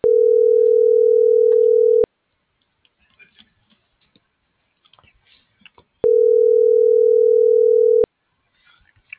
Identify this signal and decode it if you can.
Ringback tone (North America), 440 + 480 Hz.